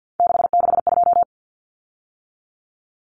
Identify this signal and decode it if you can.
Morse code, heard as a keyed tone, about 50 wpm, 700 Hz. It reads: 662
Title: Common Cut Numbers - 3-Digit 50wpm